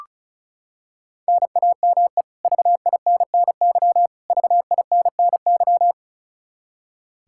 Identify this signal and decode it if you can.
Morse code, heard as a keyed tone, about 34 wpm, 700 Hz. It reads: NAME VINNY VINNY
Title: K1USN SST QSO Practice 35wpm